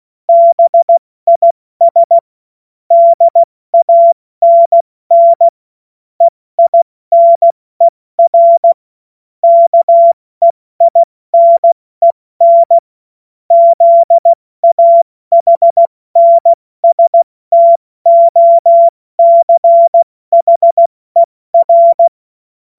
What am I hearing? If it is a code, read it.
Morse code, heard as a keyed tone, about 16 wpm, 680 Hz. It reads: BIS DANN EINER KEINEN ZAHNSTOCHER
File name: Morsekurs / Witze / Witz 02 Januar 2024 70 F80